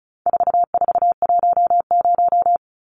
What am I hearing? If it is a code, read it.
Morse code, heard as a keyed tone, about 34 wpm, 700 Hz. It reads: 4410
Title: Numbers - 4-Digit 35wpm